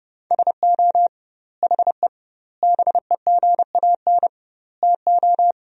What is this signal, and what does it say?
Morse code, heard as a keyed tone, about 30 wpm, 700 Hz. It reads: SO HE BEGAN TO
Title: 2x Word-Spacing - Sentences from Top 300 Words - 30wpm